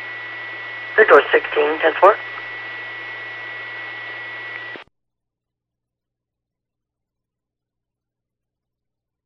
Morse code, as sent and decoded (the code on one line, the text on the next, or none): none